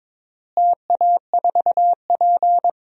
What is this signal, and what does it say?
Morse code, heard as a keyed tone, about 22 wpm, 700 Hz. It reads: TA4P
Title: DXCC-100 Call Signs - 4 Characters - Repeated 2x - 22wpm